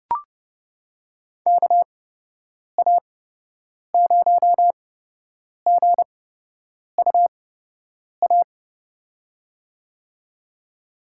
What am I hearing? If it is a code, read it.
Morse code, heard as a keyed tone, about 30 wpm, spaced as for 8 wpm, 700 Hz. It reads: KA0GUA
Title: USA Call Signs - 6 Characters - 8x Character Spacing - 30wpm